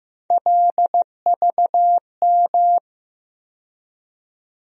Morse code, heard as a keyed tone, about 15 wpm, 700 Hz. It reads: LVM
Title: USA Call Sign Postfixes - 3 Letters 15wpm